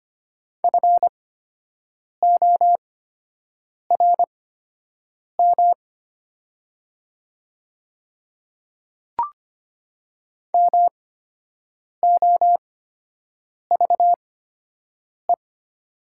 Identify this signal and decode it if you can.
Morse code, heard as a keyed tone, about 25 wpm, spaced as for 7 wpm, 700 Hz. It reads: FORM MOVE
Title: 8x Character Spacing - Top 500 Words - 25wpm